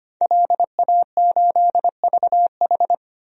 Morse code, heard as a keyed tone, about 25 wpm, 700 Hz. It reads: LA8VH
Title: DXCC-100 Call Signs - 5 Characters - Repeated 2x - 25wpm